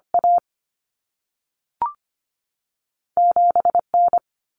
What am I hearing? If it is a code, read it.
Morse code, heard as a keyed tone, about 25 wpm, 700 Hz. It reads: A 7 N
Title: Common Cut Numbers - 2-Digit 25wpm